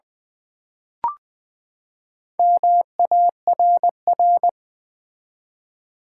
Morse code, heard as a keyed tone, about 20 wpm, 700 Hz. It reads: MARR
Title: Build A Word - 4 Letters At A Time - 8-Letter Words - 20wpm